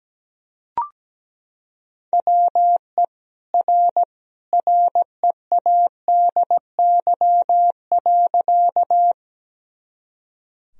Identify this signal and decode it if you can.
Morse code, heard as a keyed tone, about 17 wpm, 700 Hz. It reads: WE R READY.
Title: Sentences from Top 400 Words and Abbreviations 17wpm